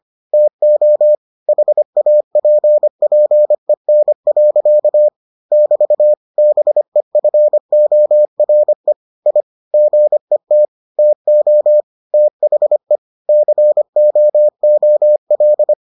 Morse code, heard as a keyed tone, about 25 wpm, 600 Hz. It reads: TO HAPPEN. <BT> BEFORE I GET TO THE COOL